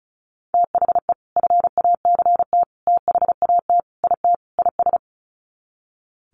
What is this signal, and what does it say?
Morse code, heard as a keyed tone, about 34 wpm, 700 Hz. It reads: THE FACT THAT IT IS
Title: Sets of 5 Words Encore2 35wpm